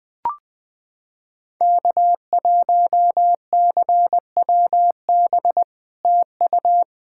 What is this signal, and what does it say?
Morse code, heard as a keyed tone, about 20 wpm, 700 Hz. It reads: K1CWB TU
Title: POTA QSO Practice - 20wpm